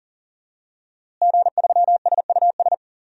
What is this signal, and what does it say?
Morse code, heard as a keyed tone, about 40 wpm, 700 Hz. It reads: G3SUS
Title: DXCC-100 Call Signs - 5 Characters - Repeated 2x - 40wpm